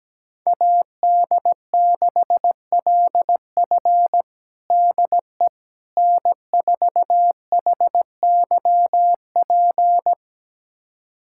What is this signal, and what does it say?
Morse code, heard as a keyed tone, about 17 wpm, 700 Hz. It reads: AD6LF DE N4HYP
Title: USA Callsign DE Callsign - 5 Characters - 17wpm